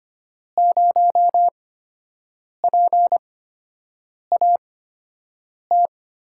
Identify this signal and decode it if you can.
Morse code, heard as a keyed tone, about 25 wpm, spaced as for 7 wpm, 700 Hz. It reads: 0PAT
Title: USA Call Signs - 6 Characters - 8x Character Spacing - 25wpm